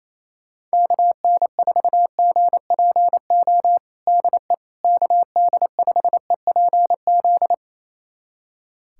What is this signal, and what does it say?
Morse code, heard as a keyed tone, about 27 wpm, 700 Hz. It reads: KN4GPO DE KD5EPZ
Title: USA Callsign DE Callsign - 6 Characters - 28wpm